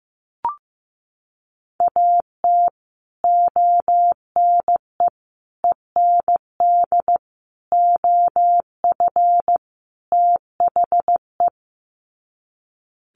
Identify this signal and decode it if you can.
Morse code, heard as a keyed tone, about 15 wpm, 700 Hz. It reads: AT ONE END OF THE
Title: Sets of 5 Words Encore 15wpm